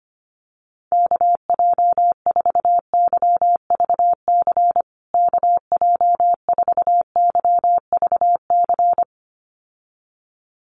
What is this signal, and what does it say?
Morse code, heard as a keyed tone, about 25 wpm, 700 Hz. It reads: KJ4YVC KJ4YVC
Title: USA Call Signs Repeated 2x 25wpm